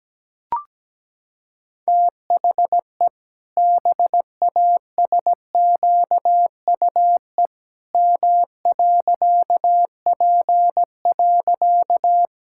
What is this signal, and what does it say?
Morse code, heard as a keyed tone, about 17 wpm, 700 Hz. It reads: THE BASQUE M.P.
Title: Pangram Challenge v2 - 17wpm